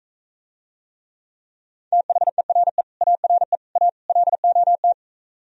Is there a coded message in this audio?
Yes: Morse code from 1.9 to 4.9 s, heard as a keyed tone, about 46 wpm, 700 Hz. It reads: THERE ARE A LOT